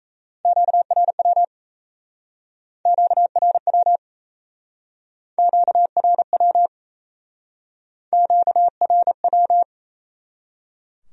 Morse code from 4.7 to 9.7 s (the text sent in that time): QRW QRW